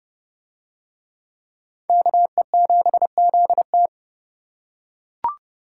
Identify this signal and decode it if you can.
Morse code, heard as a keyed tone, about 28 wpm, 700 Hz. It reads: KE7ZT